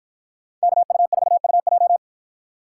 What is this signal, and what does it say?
Morse code, heard as a keyed tone, about 53 wpm, 700 Hz. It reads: KU4UJ